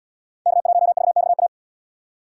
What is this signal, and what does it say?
Morse code, heard as a keyed tone, about 78 wpm, 690 Hz. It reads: K1VXA